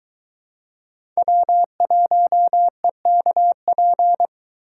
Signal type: Morse code

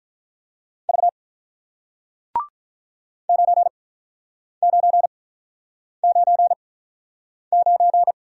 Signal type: Morse code

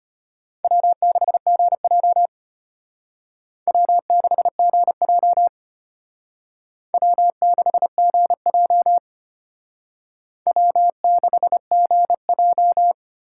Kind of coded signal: Morse code

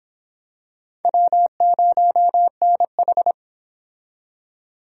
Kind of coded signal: Morse code

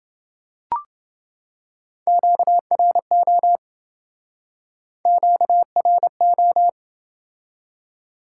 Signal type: Morse code

